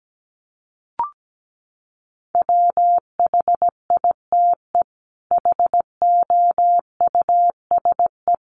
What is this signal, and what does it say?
Morse code, heard as a keyed tone, about 17 wpm, 700 Hz. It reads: WHITE HOUSE
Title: Sets of 5 Words Encore 17wpm